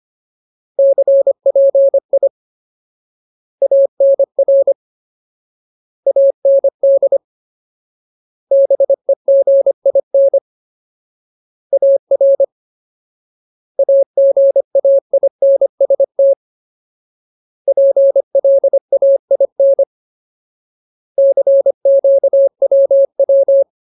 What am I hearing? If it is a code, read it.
Morse code, heard as a keyed tone, about 25 wpm, 550 Hz. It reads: CPI ANR AND BEGIN AR AGAINST PLAIN CQWW
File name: Ultimate Word List / 25WPM Ultimate